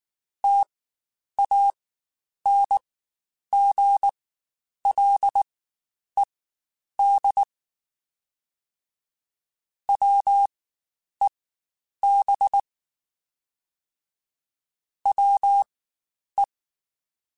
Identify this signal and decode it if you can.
Morse code, heard as a keyed tone, about 19 wpm, spaced as for 9 wpm, 800 Hz. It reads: TANGLED WEB WE
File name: Quotes Spaced / Quotes Spaced-mp3s / Deceive sp @18